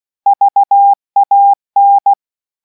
Morse code, heard as a keyed tone, about 16 wpm, 800 Hz. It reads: VAN